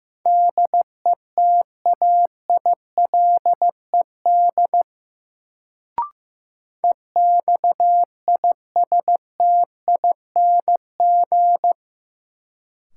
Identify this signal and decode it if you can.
Morse code, heard as a keyed tone, about 15 wpm, 700 Hz. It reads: DETAILED EXISTING
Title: Sentences from Top 1500 Words - New Word Drill Alt - 15wpm